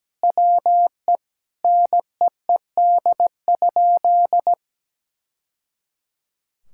Morse code, heard as a keyed tone, about 17 wpm, 700 Hz. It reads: WE NEED?
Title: Sentences from Top 1000 Words - Review Part 5 - 17wpm